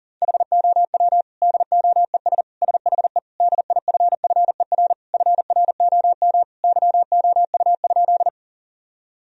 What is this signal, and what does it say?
Morse code, heard as a keyed tone, about 40 wpm, 700 Hz. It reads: HOW DOES SHE DIFFER FROM YOU?
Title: Sentences from Top 1000 Words - Review Part 1 - 40wpm